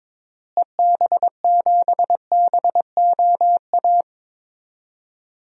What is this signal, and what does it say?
Morse code, heard as a keyed tone, about 21 wpm, 700 Hz. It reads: EB7BOA
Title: DXCC-100 Call Signs - 6 Characters - Repeated 3x - 22wpm